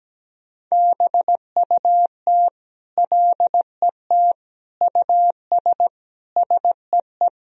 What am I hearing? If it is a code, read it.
Morse code, heard as a keyed tone, about 17 wpm, 700 Hz. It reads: BUT LET US SEE
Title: Sentences from Top 500 Words and Abbreviations 17wpm